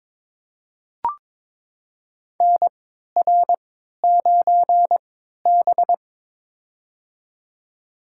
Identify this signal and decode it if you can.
Morse code, heard as a keyed tone, about 22 wpm, spaced as for 12 wpm, 700 Hz. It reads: NR9B
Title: USA Call Signs - 3x Character Spacing - 22wpm